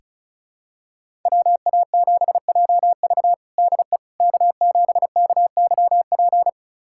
Morse code, heard as a keyed tone, about 36 wpm, 700 Hz. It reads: WA7JV DE K7KYP
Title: USA Callsign DE Callsign - 5 Characters - 35wpm